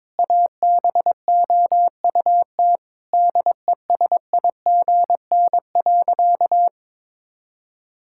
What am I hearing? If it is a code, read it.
Morse code, heard as a keyed tone, about 22 wpm, 700 Hz. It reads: ABOUT DESIGN.
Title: Sentences from Top 1000 Words - Review Part 5 - 22wpm